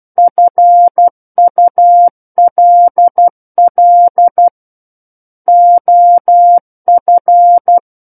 Morse code, heard as a keyed tone, about 12 wpm, 700 Hz. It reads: FULL OF
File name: Book One / 03-Chapter 3 - 12 WPM